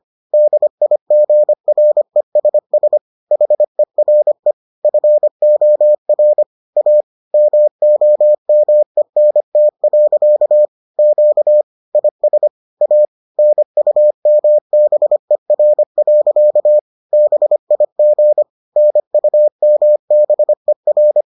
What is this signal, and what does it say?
Morse code, heard as a keyed tone, about 25 wpm, 600 Hz. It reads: DIGRESS HERE FOR A MOMENT. Q IS A NUMBER. BIG NUMBER